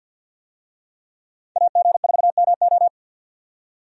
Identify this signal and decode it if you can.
Morse code, heard as a keyed tone, about 52 wpm, 700 Hz. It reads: AG4MO
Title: USA Call Signs - 5 Characters - 50wpm